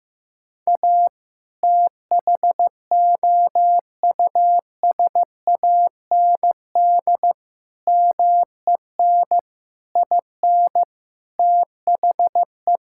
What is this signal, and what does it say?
Morse code, heard as a keyed tone, about 16 wpm, 700 Hz. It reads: A THOUSAND MEN IN THE
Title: Sentences from Top 1000 Words - Review Part 6 - 15wpm